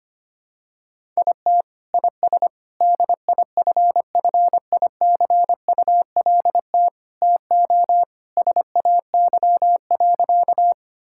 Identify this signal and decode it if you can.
Morse code, heard as a keyed tone, about 25 wpm, 700 Hz. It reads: IT IS DIFFICULT TO SAY.
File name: practice / 25wpm / Sentences with 4-Word Phrases - 25wpm